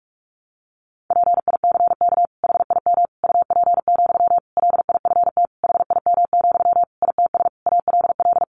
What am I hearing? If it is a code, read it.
Morse code, heard as a keyed tone, about 46 wpm, 700 Hz. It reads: PICK HIM UP, LIFT HIM, ITS ALL
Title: Sentences from Top 1000 Words 45wpm